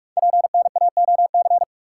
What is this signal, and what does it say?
Morse code, heard as a keyed tone, about 44 wpm, 690 Hz. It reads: PNAOC
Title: Code Groups - 5 Letters - 45wpm